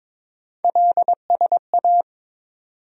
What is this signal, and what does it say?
Morse code, heard as a keyed tone, about 22 wpm, 700 Hz. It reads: LSA
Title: USA Call Sign Postfixes - 3 Letters 22wpm